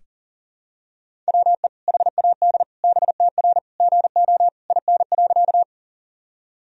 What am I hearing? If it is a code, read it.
Morse code, heard as a keyed tone, about 37 wpm, 700 Hz. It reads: WE HAD BTR GO IN.
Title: Sentences from Top 400 Words and Abbreviations 40wpm